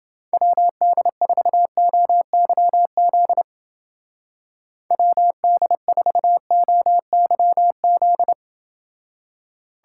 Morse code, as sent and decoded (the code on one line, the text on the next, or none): .-- -.. ....- --- -.-- --.. / .-- -.. ....- --- -.-- --..
WD4OYZ WD4OYZ